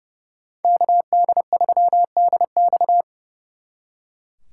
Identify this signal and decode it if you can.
Morse code, heard as a keyed tone, about 40 wpm, 700 Hz. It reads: KD3DX